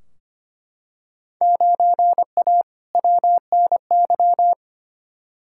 Morse code, heard as a keyed tone, about 25 wpm, 700 Hz. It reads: 9A WNY
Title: Field Day Exchanges Encore - 25wpm